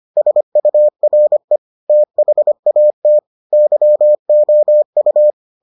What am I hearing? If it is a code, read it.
Morse code, heard as a keyed tone, about 25 wpm, 600 Hz. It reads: SURE THAT YOU